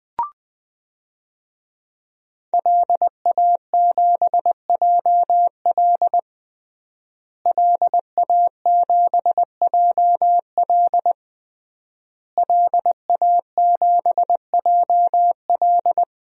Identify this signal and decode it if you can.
Morse code, heard as a keyed tone, about 20 wpm, 700 Hz. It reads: LA7JL LA7JL LA7JL
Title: DXCC-100 Call Signs - 5 Characters - Repeated 3x - 20wpm